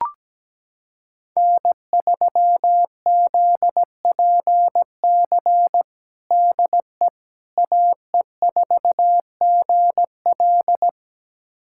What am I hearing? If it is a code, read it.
Morse code, heard as a keyed tone, about 17 wpm, 700 Hz. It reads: N3ZPC DE AE4GL
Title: USA Callsign DE Callsign - 5 Characters - 17wpm